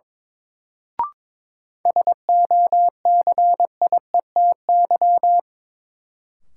Morse code, heard as a keyed tone, about 21 wpm, 700 Hz. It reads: SOCIETY
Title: Sentences from Top 700 Words - New Word Drill - 22wpm